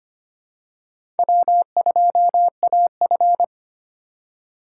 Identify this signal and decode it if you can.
Morse code, heard as a keyed tone, about 25 wpm, 700 Hz. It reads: W2AF
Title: USA Call Signs - 4 Characters - 4x Character Spacing - 25wpm